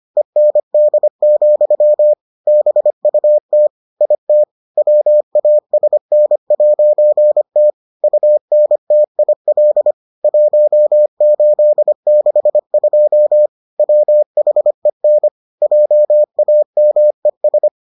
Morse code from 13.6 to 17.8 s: WHEN JAMES

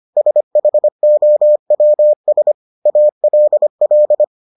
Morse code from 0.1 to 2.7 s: SHOWS

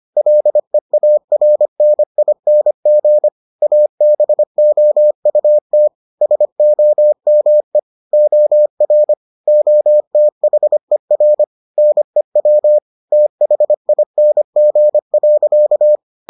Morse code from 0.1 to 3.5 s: LEARNING

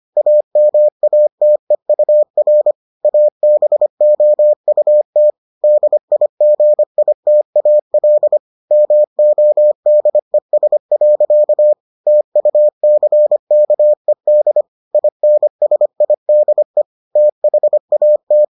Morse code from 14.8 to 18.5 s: INSIDE THAT

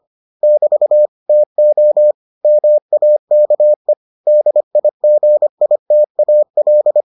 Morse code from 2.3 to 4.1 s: MAKE

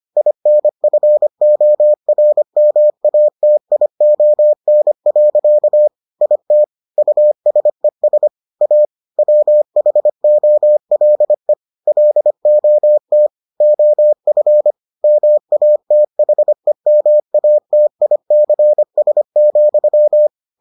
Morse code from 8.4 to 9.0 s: A

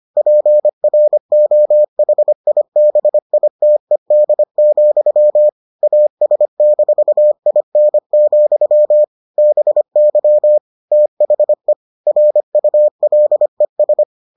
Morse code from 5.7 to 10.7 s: AS-IN, BY

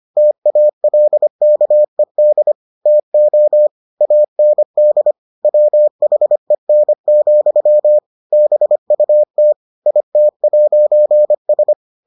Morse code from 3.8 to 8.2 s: AND WHEN,

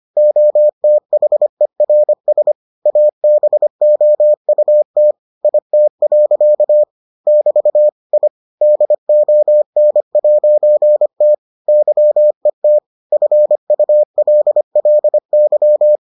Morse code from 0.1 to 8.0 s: OTHERS ABOUT IT. <BT>

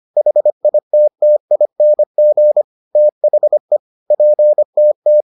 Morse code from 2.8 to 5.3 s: THE PTT

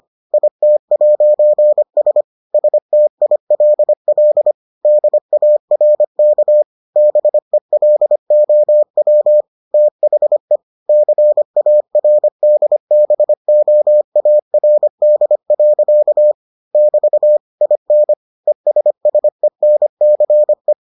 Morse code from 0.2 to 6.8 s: IT'S STILL DARK

